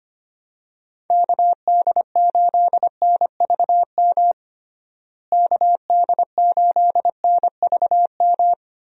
Morse code, heard as a keyed tone, about 25 wpm, 700 Hz. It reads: KD8NVM KD8NVM
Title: USA Call Signs - 6 Characters - Repeated 2x - 25wpm